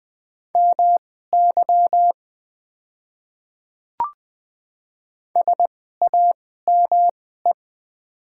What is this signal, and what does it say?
Morse code, heard as a keyed tone, about 20 wpm, spaced as for 14 wpm, 700 Hz. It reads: MY SAME
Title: 2x Character Spacing - Top 300 Words - 20wpm